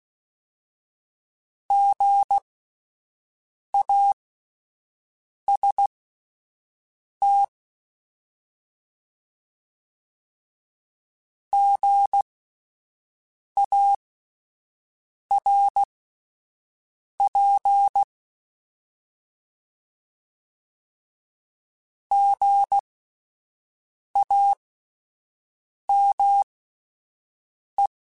Morse code, heard as a keyed tone, about 16 wpm, spaced as for 5 wpm, 800 Hz. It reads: GAST GARP GAME